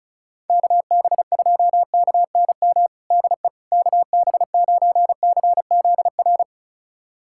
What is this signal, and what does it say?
Morse code, heard as a keyed tone, about 36 wpm, 700 Hz. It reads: KB2KNM DE KB9CZR
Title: USA Callsign DE Callsign - 6 Characters - 35wpm